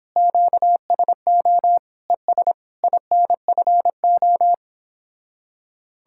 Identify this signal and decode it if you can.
Morse code, heard as a keyed tone, about 31 wpm, 700 Hz. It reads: QSO ES INFO